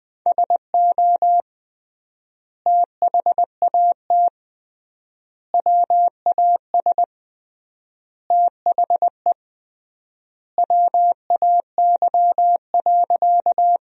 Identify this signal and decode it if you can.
Morse code, heard as a keyed tone, about 20 wpm, 700 Hz. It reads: SO THAT WAS THE WAY.
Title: 3x Word-Spacing - Sentences from Top 100 Words - 20wpm